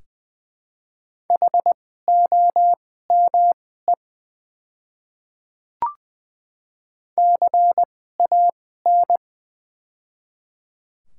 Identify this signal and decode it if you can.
Morse code, heard as a keyed tone, about 20 wpm, spaced as for 15 wpm, 700 Hz. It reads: HOME CAN
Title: 2x Character Spacing - Top 200 Words - 20wpm